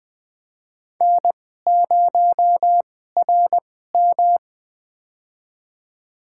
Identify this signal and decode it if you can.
Morse code, heard as a keyed tone, about 20 wpm, spaced as for 15 wpm, 700 Hz. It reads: N0RM
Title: USA Call Signs - 2x Character Spacing - 20wpm